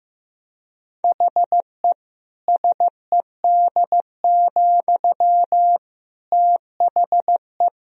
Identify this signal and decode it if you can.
Morse code, heard as a keyed tone, about 15 wpm, 700 Hz. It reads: HE SED, THE
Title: Sentences from Top 400 Words and Abbreviations 15wpm